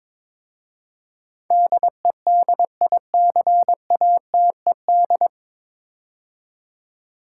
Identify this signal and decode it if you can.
Morse code, heard as a keyed tone, about 22 wpm, 700 Hz. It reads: DEDICATED
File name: 9 Letter Words - First 3 Letters Given - 22wpm